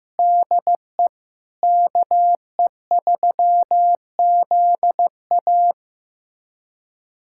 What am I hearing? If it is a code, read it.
Morse code, heard as a keyed tone, about 16 wpm, 700 Hz. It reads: DE KE3ZA
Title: USA Callsign DE Callsign - 5 Characters - 15wpm